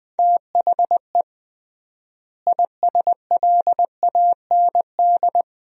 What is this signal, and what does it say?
Morse code, heard as a keyed tone, about 21 wpm, 700 Hz. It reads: THE ISLAND
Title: 3x Word-Spacing - Sentences from Top 500 Words - 20wpm